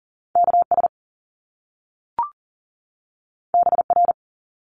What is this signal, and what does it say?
Morse code, heard as a keyed tone, about 41 wpm, 700 Hz. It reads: KS BR
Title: Top Two Letter Combinations in Words 40wpm